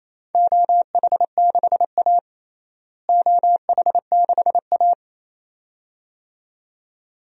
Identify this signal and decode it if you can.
Morse code, heard as a keyed tone, about 28 wpm, 700 Hz. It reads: OH6A OH6A
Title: DXCC-100 Call Signs - 4 Characters - Repeated 2x - 28wpm